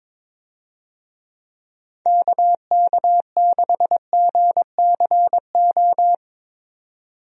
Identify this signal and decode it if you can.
Morse code, heard as a keyed tone, about 22 wpm, 700 Hz. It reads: KK6GCO
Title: USA Call Signs - 6 Characters - 22wpm